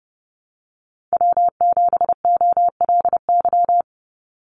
Morse code, heard as a keyed tone, about 29 wpm, 700 Hz. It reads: W7OLY
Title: Call Signs Repeated 2x - 30wpm